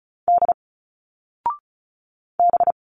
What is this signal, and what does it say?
Morse code, heard as a keyed tone, about 34 wpm, 700 Hz. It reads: D B
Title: Single Letters 35wpm